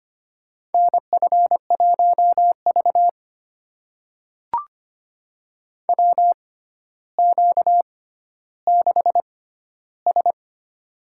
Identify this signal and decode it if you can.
Morse code, heard as a keyed tone, about 25 wpm, spaced as for 13 wpm, 700 Hz. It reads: NF1V WQ6S
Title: USA Call Signs - 4 Characters - 6x Character Spacing - 25wpm